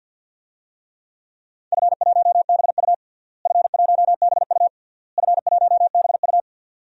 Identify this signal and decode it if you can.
Morse code, heard as a keyed tone, about 50 wpm, 700 Hz. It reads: F1BU F1BU F1BU
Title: DXCC-100 Call Signs - 4 Characters - Repeated 3x - 50wpm